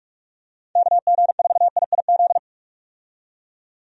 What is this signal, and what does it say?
Morse code, heard as a keyed tone, about 46 wpm, 700 Hz. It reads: KG4IIZ
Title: USA Call Signs - 6 Characters - 45wpm